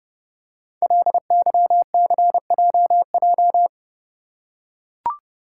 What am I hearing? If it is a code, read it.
Morse code, heard as a keyed tone, about 32 wpm, 700 Hz. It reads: LYCJJ
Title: Code Groups - 5 Letter:Numbers - 30wpm